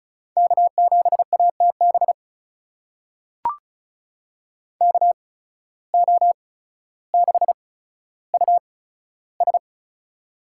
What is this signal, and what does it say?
Morse code, heard as a keyed tone, about 36 wpm, spaced as for 15 wpm, 700 Hz. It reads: K7ATB KO6US
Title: USA Call Signs - 5 Characters - 8x Character Spacing - 35wpm